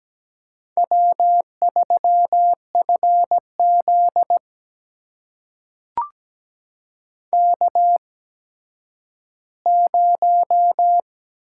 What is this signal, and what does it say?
Morse code, heard as a keyed tone, about 17 wpm, spaced as for 10 wpm, 700 Hz. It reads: W3FZ K0
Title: USA Call Signs - 4 Characters - 8x Character Spacing - 17wpm